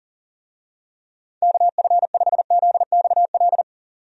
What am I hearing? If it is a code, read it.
Morse code, heard as a keyed tone, about 40 wpm, 690 Hz. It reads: KF5ZXL
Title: USA Call Signs - 6 Characters - 40wpm